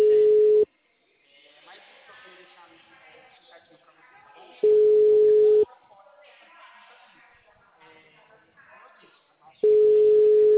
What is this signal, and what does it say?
Ringback tone (Europe, 425 Hz), with speech.
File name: out- -2014- - - .10188